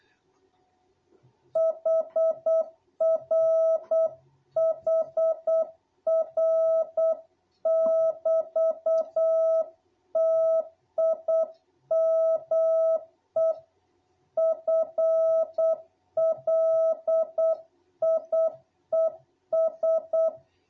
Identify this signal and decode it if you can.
Morse code, heard as a keyed tone, about 8 wpm, 660 Hz. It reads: HR HR<BT>TIME FLIES
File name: jh8cbh / 2021cw